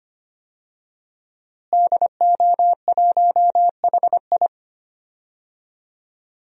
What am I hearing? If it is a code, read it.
Morse code, heard as a keyed tone, about 25 wpm, 700 Hz. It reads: DO1HI
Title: DXCC-100 Call Signs - 5 Characters - Repeated 2x - 25wpm